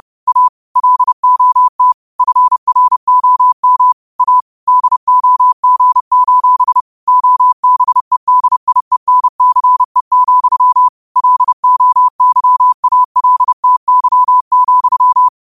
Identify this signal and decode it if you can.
Morse code, heard as a keyed tone, about 30 wpm, 1000 Hz. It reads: A LOT FROM A DOG: OBEDIENCE, LOYALTY,